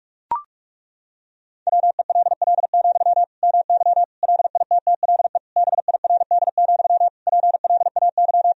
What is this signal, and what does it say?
Morse code, heard as a keyed tone, about 41 wpm, 700 Hz. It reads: WELL, MY LITTLE BIRD, PLAY
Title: Sentences from Top 1000 Words - Review Part 2 - 45wpm